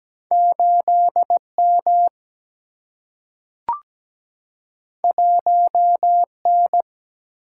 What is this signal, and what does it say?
Morse code, heard as a keyed tone, about 17 wpm, 700 Hz. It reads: 8 M 1 N
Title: Single-Letter Single-Number 17wpm